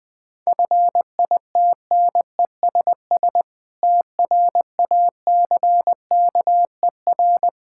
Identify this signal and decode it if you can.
Morse code, heard as a keyed tone, about 20 wpm, 700 Hz. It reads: FITNESS TRACKER
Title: Difficult Words - Speed-Racing with Hints - 20wpm